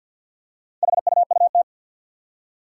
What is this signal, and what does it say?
Morse code, heard as a keyed tone, about 47 wpm, 700 Hz. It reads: HUUT
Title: Four Challenging Letter-Number Combinations 50wpm